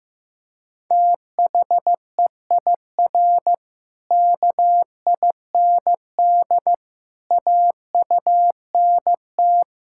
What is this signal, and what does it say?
Morse code, heard as a keyed tone, about 15 wpm, 700 Hz. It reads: THEIR KIND AUNT
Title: Pangram Challenge v2 - 15wpm